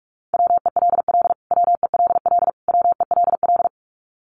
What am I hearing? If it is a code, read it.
Morse code, heard as a keyed tone, about 44 wpm, 700 Hz. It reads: WELL WELL WELL
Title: Top 500 Words Repeated 45wpm